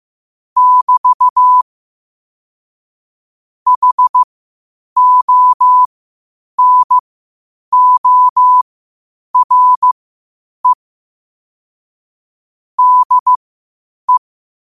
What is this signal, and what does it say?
Morse code, heard as a keyed tone, about 15 wpm, spaced as for 8 wpm, 1000 Hz. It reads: <BT> HONORE DE
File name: qotdfiles / 8wpm